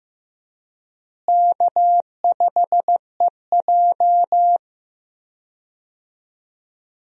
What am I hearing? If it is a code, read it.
Morse code, heard as a keyed tone, about 15 wpm, 700 Hz. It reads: K5EJ
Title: USA Call Signs - 4 Characters - Repeated 3x - 15wpm